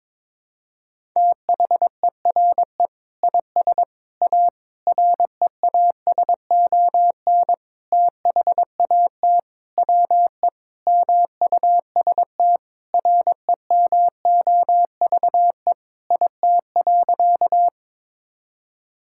Morse code, heard as a keyed tone, about 22 wpm, 700 Hz. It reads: THERE IS A REASON THAT WE MUST REMOVE IT.
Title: Sentences from Top 1000 Words - Review Part 5 - 22wpm